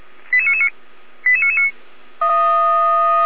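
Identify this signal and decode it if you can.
Five-tone selective call (ZVEI), heard as quick runs of five tones, tones of about 70 ms. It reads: 88999 88999
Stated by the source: Five-tone ZVEI selective call received on an SDR.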